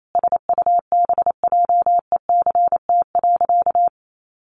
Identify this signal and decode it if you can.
Morse code, heard as a keyed tone, about 28 wpm, 700 Hz. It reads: SUBJECT.
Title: Sentences from Top 800 Words 28wpm